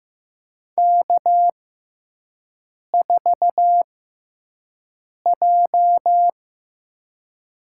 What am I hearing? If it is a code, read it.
Morse code, heard as a keyed tone, about 15 wpm, spaced as for 5 wpm, 700 Hz. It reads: K4J
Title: USA Call Signs - 5 Characters - 6x Character Spacing - 15wpm